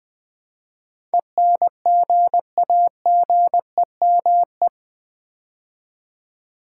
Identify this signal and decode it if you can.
Morse code, heard as a keyed tone, about 20 wpm, 700 Hz. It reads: ENGAGEME
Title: Build A Word - 4 Letters At A Time - 10-Letter Words - 20wpm